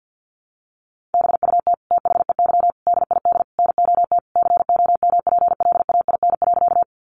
Morse code, heard as a keyed tone, about 50 wpm, 700 Hz. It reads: BUT THEY DID NOT COMPLAIN.
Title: Sentences from Top 1300 Words 50wpm